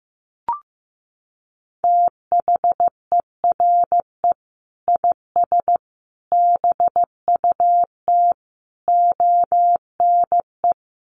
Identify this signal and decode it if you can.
Morse code, heard as a keyed tone, about 15 wpm, 700 Hz. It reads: THERE IS BUT ONE
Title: Sentences from Top 100 Words 15wpm